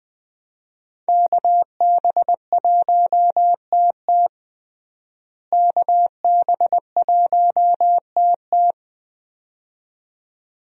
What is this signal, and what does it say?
Morse code, heard as a keyed tone, about 20 wpm, 700 Hz. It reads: KB1TT KB1TT
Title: USA Call Signs - 5 Characters - Repeated 2x - 20wpm